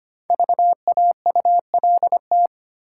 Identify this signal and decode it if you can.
Morse code, heard as a keyed tone, about 25 wpm, 690 Hz. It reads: VAULT
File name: practice / 25wpm / Easy Words with Hints - 25wpm